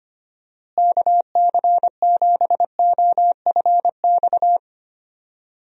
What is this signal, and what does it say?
Morse code, heard as a keyed tone, about 25 wpm, 700 Hz. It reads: KC7OFX
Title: USA Call Signs - 6 Characters - Repeated 3x - 25wpm